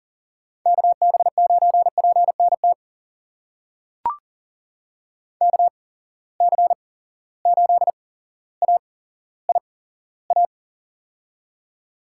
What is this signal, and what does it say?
Morse code, heard as a keyed tone, about 39 wpm, spaced as for 17 wpm, 700 Hz. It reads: KB9PNT KC8AIA
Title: USA Call Signs - 6 Characters - 8x Character Spacing - 40wpm